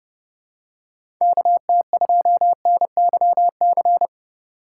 Morse code, heard as a keyed tone, about 30 wpm, 700 Hz. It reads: KT2NYC
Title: USA Call Signs - 6 Characters - 3x Character Spacing - 30wpm